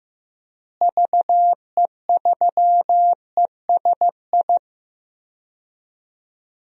Morse code, heard as a keyed tone, about 15 wpm, 700 Hz. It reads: VE3ESI
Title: DXCC-100 Call Signs - 6 Characters - Repeated 2x - 15wpm